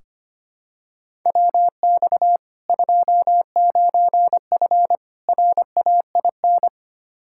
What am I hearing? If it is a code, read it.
Morse code, heard as a keyed tone, about 25 wpm, 700 Hz. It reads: WX 29F RAIN
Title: K1USN SST QSO Practice 25wpm